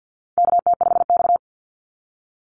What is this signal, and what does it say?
Morse code, heard as a keyed tone, about 50 wpm, 700 Hz. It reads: KT5X
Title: NA SOTA Activator Callsigns 50wpm